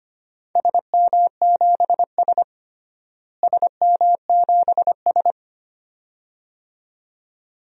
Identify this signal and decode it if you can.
Morse code, heard as a keyed tone, about 25 wpm, 700 Hz. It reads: SM7S SM7S
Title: DXCC-100 Call Signs - 4 Characters - Repeated 2x - 25wpm